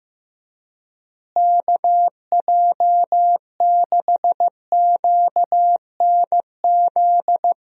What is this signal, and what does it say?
Morse code, heard as a keyed tone, about 15 wpm, 700 Hz. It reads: KJ6QNZ
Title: USA Call Signs - 6 Characters - Repeated 2x - 15wpm